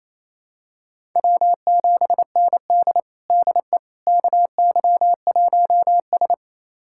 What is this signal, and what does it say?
Morse code, heard as a keyed tone, about 27 wpm, 700 Hz. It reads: W7ND DE KY1S
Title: USA Callsign DE Callsign - 4 Characters - 28wpm